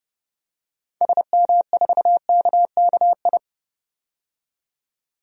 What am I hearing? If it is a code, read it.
Morse code, heard as a keyed tone, about 29 wpm, 690 Hz. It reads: SM4KKI
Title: DXCC-100 Call Signs - 6 Characters - 30wpm